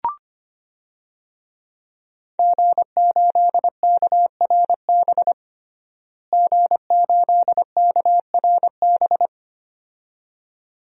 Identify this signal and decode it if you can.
Morse code, heard as a keyed tone, about 25 wpm, 700 Hz. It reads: G8KRB G8KRB
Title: DXCC-100 Call Signs - 5 Characters - Repeated 2x - 25wpm